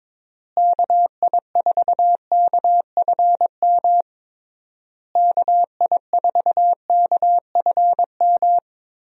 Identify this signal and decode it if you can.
Morse code, heard as a keyed tone, about 21 wpm, 700 Hz. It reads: KI4KFM KI4KFM
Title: USA Call Signs - 6 Characters - Repeated 3x - 22wpm